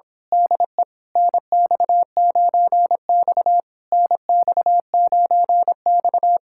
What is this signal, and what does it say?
Morse code, heard as a keyed tone, about 26 wpm, 700 Hz. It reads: DE NX9X NX9X
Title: Speed-Racing Common QSO Phrases - 22wpm